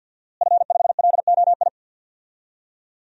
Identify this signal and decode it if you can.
Morse code, heard as a keyed tone, about 51 wpm, 700 Hz. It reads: F5LOI
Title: DXCC-100 Call Signs - 5 Characters - Repeated 3x - 50wpm